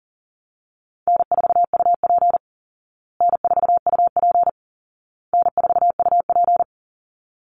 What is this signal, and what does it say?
Morse code, heard as a keyed tone, about 40 wpm, 700 Hz. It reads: N4UP N4UP N4UP
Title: Easy Call Signs Repeated 3x - 40wpm